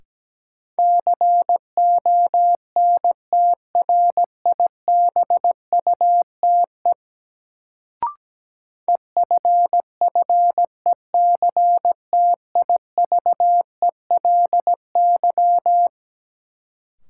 Morse code, heard as a keyed tone, about 17 wpm, 700 Hz. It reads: CONTRIBUTE EFFECTIVELY
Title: Sentences from Top 1400 Words - New Word Drill Alt - 17wpm